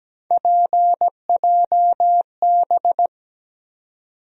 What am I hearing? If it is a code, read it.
Morse code, heard as a keyed tone, about 17 wpm, 700 Hz. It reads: PJB
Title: USA Call Sign Postfixes - 3 Letters 17wpm